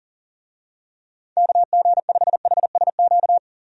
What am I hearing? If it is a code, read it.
Morse code, heard as a keyed tone, about 40 wpm, 700 Hz. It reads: KG5HSQ
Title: USA Call Signs - 6 Characters - 40wpm